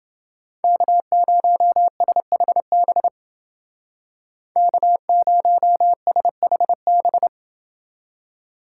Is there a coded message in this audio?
Yes: Morse code from 0.6 to 7.3 s, heard as a keyed tone, about 29 wpm, 700 Hz. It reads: K0SHB K0SHB